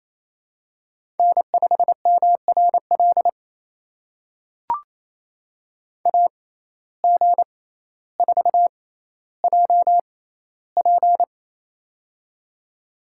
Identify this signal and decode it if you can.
Morse code, heard as a keyed tone, about 27 wpm, spaced as for 14 wpm, 700 Hz. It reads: N5MRL AG4JP
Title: USA Call Signs - 5 Characters - 6x Character Spacing - 28wpm